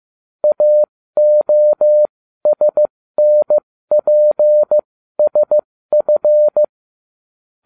Morse code, heard as a keyed tone, about 15 wpm, spaced as for 13 wpm, 600 Hz. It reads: AOSNPSF